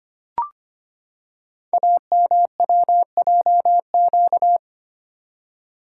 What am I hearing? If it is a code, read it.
Morse code, heard as a keyed tone, about 25 wpm, 700 Hz. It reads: AMWJQ
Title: Code Groups - 5 Letters - 25wpm